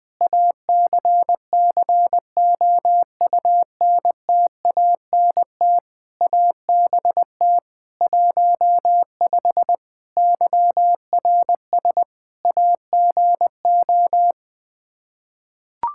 Morse code, heard as a keyed tone, about 20 wpm, 700 Hz. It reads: ACCOUNTANT ABT 15 YRS AGO
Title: Practice QSO - Exchange 3 - More Info - 20wpm